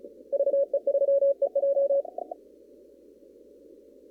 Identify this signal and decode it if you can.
Morse code, heard as a keyed tone, about 35 wpm, 560 Hz. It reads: VE3EJ
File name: CW / kt5j wpx cw-op-k5pi